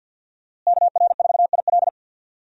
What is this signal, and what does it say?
Morse code, heard as a keyed tone, about 46 wpm, 700 Hz. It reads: KR4IL
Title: USA Call Signs - 5 Characters - Repeated 2x - 50wpm